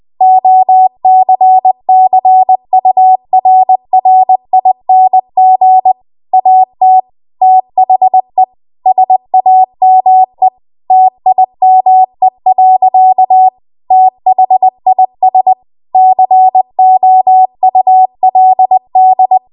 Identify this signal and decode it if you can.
Morse code, heard as a keyed tone, about 20 wpm, 750 Hz. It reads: OCCURRING AT THE SAME TIME. THIS COULD